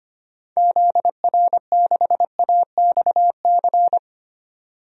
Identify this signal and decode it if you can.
Morse code, heard as a keyed tone, about 25 wpm, 700 Hz. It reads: ZR6AXC
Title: DXCC-100 Call Signs - 6 Characters - 25wpm